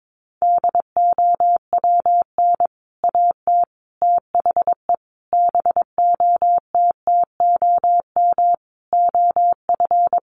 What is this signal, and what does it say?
Morse code, heard as a keyed tone, about 22 wpm, 700 Hz. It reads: DOWN AT THE BOTTOM OF
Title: Sentences from Top 1100 Words 22wpm